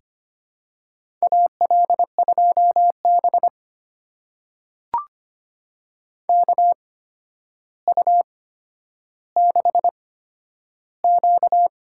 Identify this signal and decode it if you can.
Morse code, heard as a keyed tone, about 25 wpm, spaced as for 11 wpm, 700 Hz. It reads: AL2B KU6Q
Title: USA Call Signs - 4 Characters - 8x Character Spacing - 25wpm